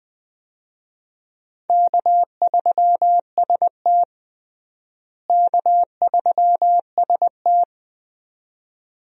Morse code, heard as a keyed tone, about 20 wpm, 700 Hz. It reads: K3ST K3ST
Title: USA Call Signs - 4 Characters - Repeated 2x - 20wpm